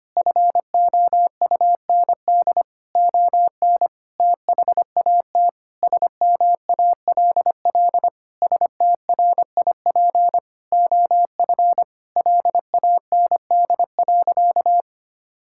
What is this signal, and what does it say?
Morse code, heard as a keyed tone, about 25 wpm, 700 Hz. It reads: FOUND ON THAT SMALL STRIP OF LAND.
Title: Sentences from Top 1600 Words 25wpm